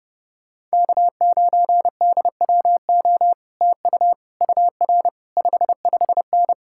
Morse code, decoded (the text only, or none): K9DWO TU UR 55N